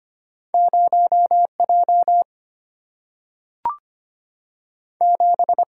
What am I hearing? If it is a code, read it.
Morse code, heard as a keyed tone, about 25 wpm, 700 Hz. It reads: 0 J 7
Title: Single-Letter Single-Number 25wpm